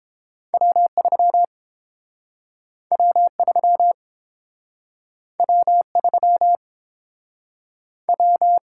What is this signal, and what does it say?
Morse code, heard as a keyed tone, about 30 wpm, 700 Hz. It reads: W3 W3 W3 W